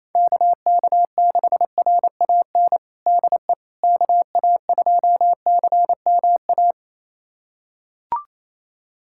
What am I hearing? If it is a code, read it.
Morse code, heard as a keyed tone, about 28 wpm, 700 Hz. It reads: KK6RAN DE KA2CMA
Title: USA Callsign DE Callsign - 6 Characters - 28wpm